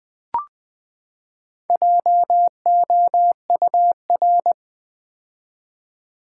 Morse code, heard as a keyed tone, about 20 wpm, 700 Hz. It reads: JOUR
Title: Build A Word - 4 Letters At A Time - 10-Letter Words - 20wpm